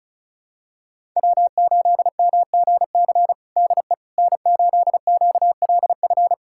Morse code, heard as a keyed tone, about 34 wpm, 700 Hz. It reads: W8MGC DE N8QLF
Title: USA Callsign DE Callsign - 5 Characters - 35wpm